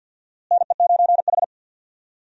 Morse code, heard as a keyed tone, about 49 wpm, 700 Hz. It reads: NE9H
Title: USA Call Signs - 4 Characters - Repeated 3x - 50wpm